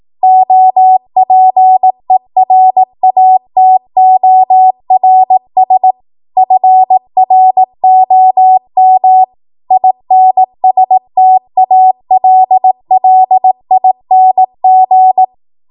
Morse code, heard as a keyed tone, about 18 wpm, 750 Hz. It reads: OPERATORS FROM INSTALLING
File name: Morse / 18WPM